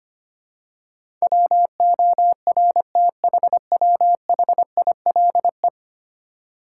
Morse code, heard as a keyed tone, about 25 wpm, 700 Hz. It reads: WORTHWHILE
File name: 10 Letter Words - First 3 Letters Given - 25wpm